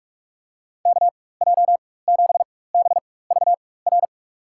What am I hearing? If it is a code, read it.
Morse code, heard as a keyed tone, about 47 wpm, spaced as for 21 wpm, 700 Hz. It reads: KJ7BVR
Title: USA Call Signs - 6 Characters - 4x Character Spacing - 45wpm